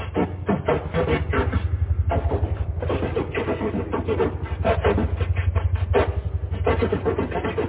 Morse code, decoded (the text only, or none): none